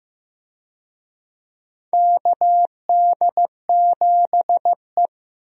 Morse code, heard as a keyed tone, about 15 wpm, 700 Hz. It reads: KD7E
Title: USA Call Signs - 4 Characters - Repeated 2x - 15wpm